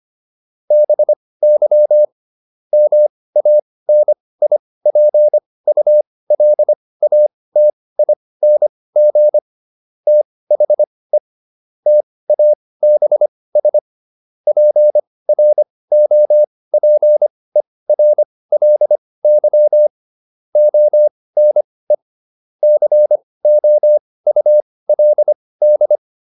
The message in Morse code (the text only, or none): BY MANIPULATING THE TABS PROPERLY ONE COULD